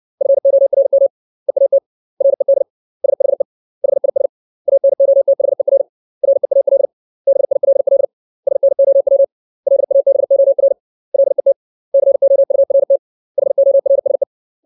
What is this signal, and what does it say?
Morse code, heard as a keyed tone, about 60 wpm, 550 Hz. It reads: FORM EAT RED SHE HIS ANOTHER REAL BILL SNOW LABOR LET COURT HORSE
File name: 632 words / 632 Words 60WPM